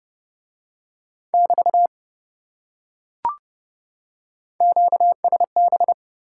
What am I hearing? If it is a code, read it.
Morse code, heard as a keyed tone, about 28 wpm, 700 Hz. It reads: <BT> QSB